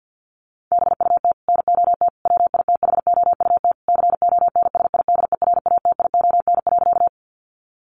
Morse code, heard as a keyed tone, about 50 wpm, 700 Hz. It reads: BUT NOT WITHOUT CONSIDERATION.
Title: Sentences from Top 1500 Words 50wpm